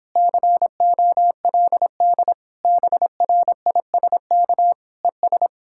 Morse code, heard as a keyed tone, about 29 wpm, 700 Hz. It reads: COLD BRISK ES